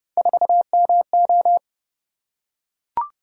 Morse code, heard as a keyed tone, about 28 wpm, 700 Hz. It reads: 4MO